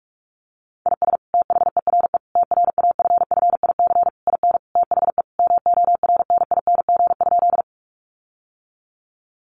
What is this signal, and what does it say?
Morse code, heard as a keyed tone, about 46 wpm, 700 Hz. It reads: IS THERE TRAFFIC IN THE MORNING?
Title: Sentences from Top 1300 Words 45wpm